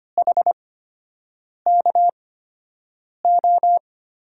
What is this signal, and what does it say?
Morse code, heard as a keyed tone, about 25 wpm, spaced as for 7 wpm, 700 Hz. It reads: HKO
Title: USA Call Signs - 6 Characters - 8x Character Spacing - 25wpm